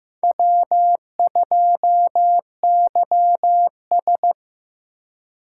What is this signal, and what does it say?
Morse code, heard as a keyed tone, about 18 wpm, 700 Hz. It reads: W2YS
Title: Speed Racing - USA Call Signs 15wpm